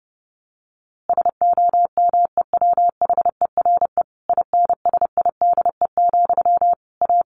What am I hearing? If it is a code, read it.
Morse code, heard as a keyed tone, about 30 wpm, 700 Hz. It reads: SOMEWHERE INSIDE, A
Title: A Sound Of Thunder 30wpm